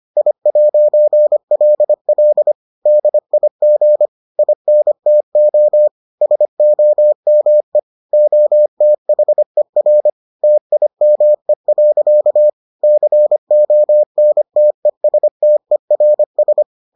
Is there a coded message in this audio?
Yes: Morse code from 0.2 to 16.6 s, heard as a keyed tone, about 25 wpm, 600 Hz. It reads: I'LL DIG INTO SOME OTHER TIME. CONTESTERS